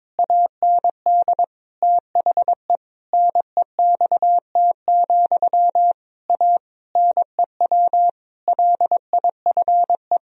Morse code, heard as a keyed tone, about 22 wpm, 700 Hz. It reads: AND THE NEXT, A NEW LIFE
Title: Sentences from Top 1000 Words - Review Part 1 - 22wpm